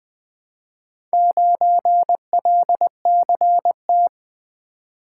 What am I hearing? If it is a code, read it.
Morse code, heard as a keyed tone, about 20 wpm, 700 Hz. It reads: 9LCT
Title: Four Letter-Number Combinations - 20wpm